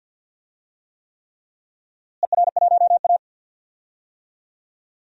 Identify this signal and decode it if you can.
Morse code, heard as a keyed tone, about 50 wpm, 700 Hz. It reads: ER1A
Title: DXCC-100 Call Signs - 4 Characters - Repeated 3x - 50wpm